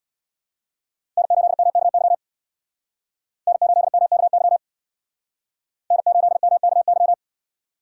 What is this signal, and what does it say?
Morse code, heard as a keyed tone, about 62 wpm, 700 Hz. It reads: N8MKY N8MKY N8MKY
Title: Speed Racing - USA 1x3 Call Signs 50wpm